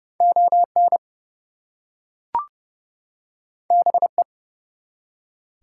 Morse code, heard as a keyed tone, about 29 wpm, 700 Hz. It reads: ON BE
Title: Two Letter Words - 30wpm